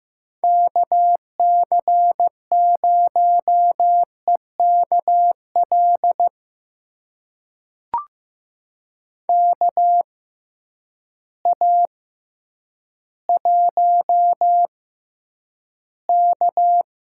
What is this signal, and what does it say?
Morse code, heard as a keyed tone, about 15 wpm, spaced as for 9 wpm, 700 Hz. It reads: KC0EKL KA1K
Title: USA Call Signs - 6 Characters - 6x Character Spacing - 15wpm